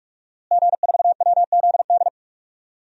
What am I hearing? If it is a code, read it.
Morse code, heard as a keyed tone, about 46 wpm, 700 Hz. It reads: G4WZD
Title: DXCC-100 Call Signs - 5 Characters - Repeated 2x - 45wpm